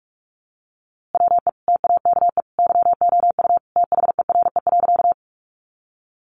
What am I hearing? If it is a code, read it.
Morse code, heard as a keyed tone, about 44 wpm, 700 Hz. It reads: WE TAKE YOU THERE.
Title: A Sound Of Thunder 45wpm